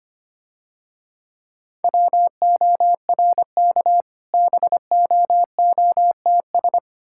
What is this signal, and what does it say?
Morse code, heard as a keyed tone, about 27 wpm, 700 Hz. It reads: WORK BOOTS
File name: practice / 25wpm / Difficult Words - Speed-Racing with Hints - 25wpm